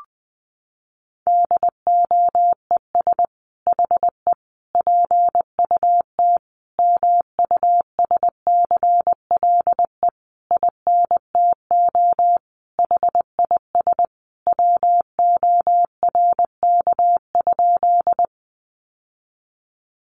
Morse code, heard as a keyed tone, about 20 wpm, 700 Hz. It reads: DOES HE PUT MUSCLE INTO HIS WORK?
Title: Sentences from Top 1300 Words 20wpm